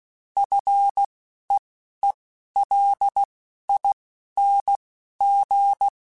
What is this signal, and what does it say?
Morse code, heard as a keyed tone, about 16 wpm, spaced as for 11 wpm, 800 Hz. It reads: FEELING
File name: F and L / F and L mp3 / F and L Prefixes 3x2 Twice @10